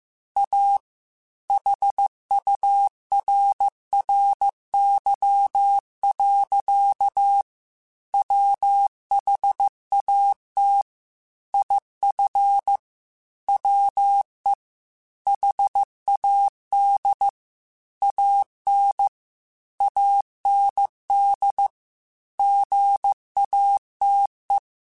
Morse code, heard as a keyed tone, about 15 wpm, 800 Hz. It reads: A HURRY. WHAT IF WE HAD AN AND GATE
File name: Technical Subjects / Technical Subjects-mp3s / AND Gate @13